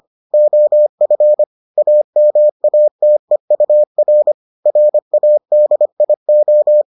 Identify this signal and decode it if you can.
Morse code, heard as a keyed tone, about 25 wpm, 600 Hz. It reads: OF AMATEUR RADIO